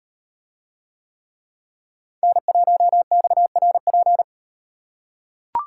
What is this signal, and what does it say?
Morse code, heard as a keyed tone, about 37 wpm, 700 Hz. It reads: N1XRP